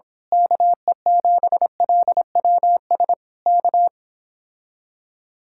Morse code, heard as a keyed tone, about 26 wpm, 700 Hz. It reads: KE7LWS K
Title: Speed-Racing Common QSO Phrases - 22wpm